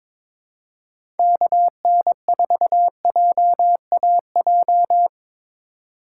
Morse code, heard as a keyed tone, about 22 wpm, 700 Hz. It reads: KN4JAJ
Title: USA Call Signs - 6 Characters - 8x Character Spacing - 22wpm